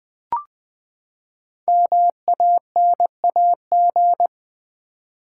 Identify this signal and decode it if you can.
Morse code, heard as a keyed tone, about 20 wpm, 700 Hz. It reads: MANAG
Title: Build A Word - 5 Letters At A Time - 10-Letter Words - 20wpm